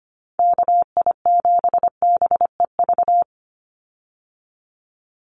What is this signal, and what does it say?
Morse code, heard as a keyed tone, about 25 wpm, 700 Hz. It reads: KI7BEV
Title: USA Call Signs 25wpm